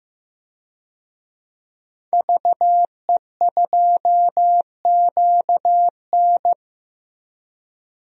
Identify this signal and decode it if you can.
Morse code, heard as a keyed tone, about 15 wpm, 700 Hz. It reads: VE2QN
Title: DXCC-100 Call Signs - 5 Characters - 15wpm